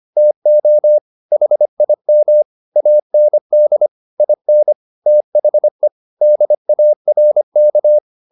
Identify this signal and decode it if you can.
Morse code, heard as a keyed tone, about 25 wpm, 600 Hz. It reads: TO HIM AND IN THE DARK
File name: the old man and the sea part 1 25wpm